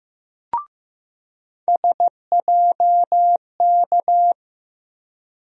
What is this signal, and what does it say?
Morse code, heard as a keyed tone, about 15 wpm, 700 Hz. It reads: SJK
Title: USA Call Sign Postfixes - 3 Letters 15wpm